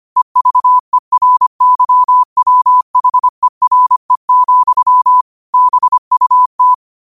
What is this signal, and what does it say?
Morse code, heard as a keyed tone, about 25 wpm, 1000 Hz. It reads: EVERYWHERE, BUT
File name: qotdfiles / 25wpm